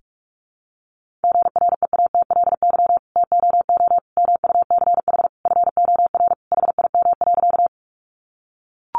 Morse code, heard as a keyed tone, about 44 wpm, 700 Hz. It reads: GREATLY TOO MUCH FOR HIM.
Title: Sentences from Top 1400 Words 45wpm